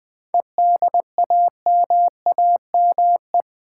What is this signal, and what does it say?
Morse code, heard as a keyed tone, about 20 wpm, 700 Hz. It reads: EDAMAME
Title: Difficult Words with Hints - 20wpm